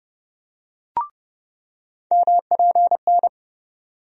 Morse code, heard as a keyed tone, about 30 wpm, 700 Hz. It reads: MPN
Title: USA Call Sign Postfixes - 3 Letters 30wpm